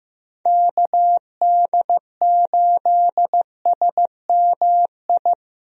Morse code, heard as a keyed tone, about 15 wpm, 700 Hz. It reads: KD8SMI
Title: USA Call Signs - 6 Characters - 15wpm